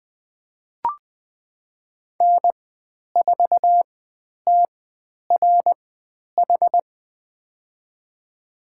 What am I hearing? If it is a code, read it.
Morse code, heard as a keyed tone, about 20 wpm, spaced as for 10 wpm, 700 Hz. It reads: N4TRH
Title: 095 Lesson 16g - Letter c - Call Signs 20wpm